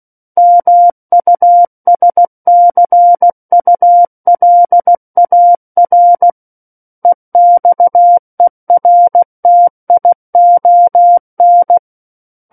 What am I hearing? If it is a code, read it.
Morse code, heard as a keyed tone, about 16 wpm, 700 Hz. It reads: MUSCULAR EXERTION
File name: Book One / 07-Chapter 7 - 16 WPM